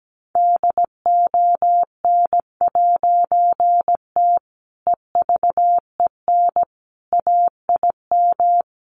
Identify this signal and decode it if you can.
Morse code, heard as a keyed tone, about 17 wpm, 700 Hz. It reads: DON'T EVEN AIM
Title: A Sound Of Thunder 17wpm